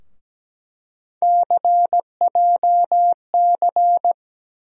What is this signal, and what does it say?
Morse code, heard as a keyed tone, about 17 wpm, 700 Hz. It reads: CJC
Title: USA Call Sign Postfixes - 3 Letters 17wpm